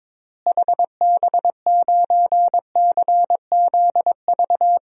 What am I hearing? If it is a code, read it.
Morse code, heard as a keyed tone, about 22 wpm, 700 Hz. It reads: HB9CZV
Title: DXCC-100 Call Signs - 6 Characters - Repeated 3x - 22wpm